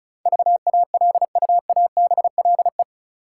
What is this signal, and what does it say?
Morse code, heard as a keyed tone, about 36 wpm, 700 Hz. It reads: VALUABLE
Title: Sentences from Top 1100 Words - New Word Drill Alt - 35wpm